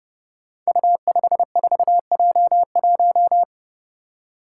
Morse code, heard as a keyed tone, about 30 wpm, 700 Hz. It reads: U54J1
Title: Five Challenging Letter-Number Combinations 30wpm